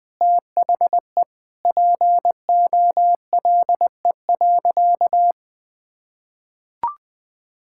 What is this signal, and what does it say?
Morse code, heard as a keyed tone, about 20 wpm, 700 Hz. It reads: THE POLE.
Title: Sentences with 5-Word Phrases - 20wpm